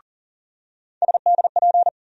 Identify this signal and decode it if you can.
Morse code, heard as a keyed tone, about 39 wpm, 700 Hz. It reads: SDP
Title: USA Call Sign Postfixes - 3 Letters 40wpm